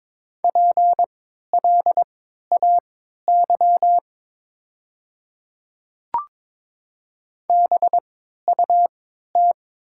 Morse code, heard as a keyed tone, about 22 wpm, spaced as for 13 wpm, 700 Hz. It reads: PLAY BUT
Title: 3x Character Spacing - Top 500 Words - 22wpm